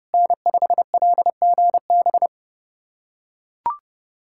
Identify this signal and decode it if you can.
Morse code, heard as a keyed tone, about 30 wpm, 700 Hz. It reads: N5LGB
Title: USA Callsign DE Callsign - 5 Characters - 30wpm